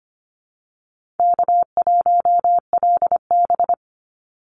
Morse code, heard as a keyed tone, about 25 wpm, 690 Hz. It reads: K1LB
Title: NA SOTA Activator Callsigns 25wpm